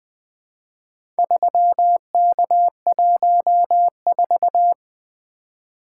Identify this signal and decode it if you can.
Morse code, heard as a keyed tone, about 20 wpm, 700 Hz. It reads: 3K14
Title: Four Letter-Number Combinations - 20wpm